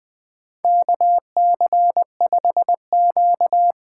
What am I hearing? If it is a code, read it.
Morse code, heard as a keyed tone, about 20 wpm, 700 Hz. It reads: KC5Q
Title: USA Call Signs - 8x Character Spacing - 20wpm